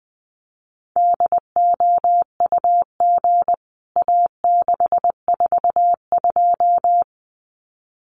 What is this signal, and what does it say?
Morse code, heard as a keyed tone, about 20 wpm, 700 Hz. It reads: DOUG A642
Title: CWT Exchange 20wpm